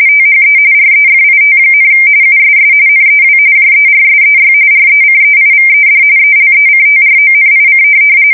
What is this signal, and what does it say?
RTTY, about 45.45 baud. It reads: THE REFIT, USS NEW JERSEY RECEIVED ELECTRONIC
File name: fsk170 / USS-New-Jersey